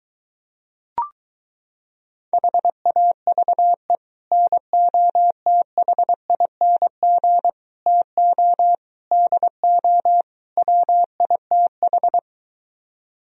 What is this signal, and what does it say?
Morse code, heard as a keyed tone, about 23 wpm, 700 Hz. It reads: HAVE NOTHING TO DO WITH